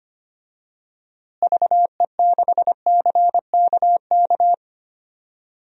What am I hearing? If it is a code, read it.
Morse code, heard as a keyed tone, about 25 wpm, 700 Hz. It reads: VE6CKK
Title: DXCC-100 Call Signs - 6 Characters - Repeated 3x - 25wpm